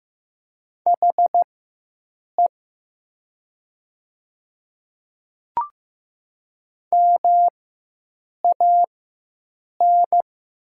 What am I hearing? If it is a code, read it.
Morse code, heard as a keyed tone, about 15 wpm, spaced as for 7 wpm, 700 Hz. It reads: HE MAN
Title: 4x Character Spacing - Top 200 Words - 15wpm